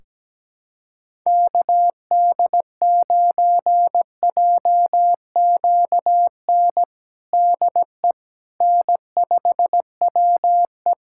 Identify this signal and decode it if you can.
Morse code, heard as a keyed tone, about 17 wpm, 700 Hz. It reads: KD9JQN DE N5WE
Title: Practice QSO - Exchange 4 - Ending - 17wpm